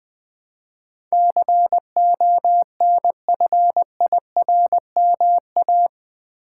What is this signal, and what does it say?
Morse code, heard as a keyed tone, about 20 wpm, 700 Hz. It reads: CONFIRMA
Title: Build A Word - 4 Letters At A Time - 12-Letter Words - 20wpm